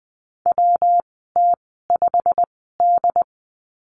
Morse code, heard as a keyed tone, about 20 wpm, spaced as for 15 wpm, 700 Hz. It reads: WT5D
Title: 173 Lesson 28g - Number 9 - Call Signs 20wpm